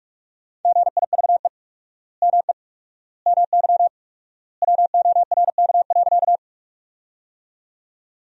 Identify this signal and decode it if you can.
Morse code, heard as a keyed tone, about 41 wpm, 700 Hz. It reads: GIVE ME MY WORK.
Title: 4x Word-Spacing - Sentences from Top 200 Words - 45wpm